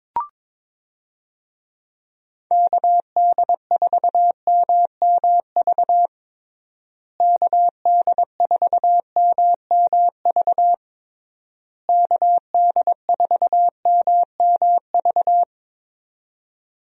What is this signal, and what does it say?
Morse code, heard as a keyed tone, about 21 wpm, 700 Hz. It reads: KD4MMV KD4MMV KD4MMV
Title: USA Call Signs - 6 Characters - Repeated 3x - 22wpm